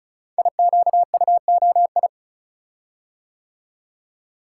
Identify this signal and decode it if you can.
Morse code, heard as a keyed tone, about 36 wpm, 690 Hz. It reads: IQUOI
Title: Code Groups - 5 Letters - 35wpm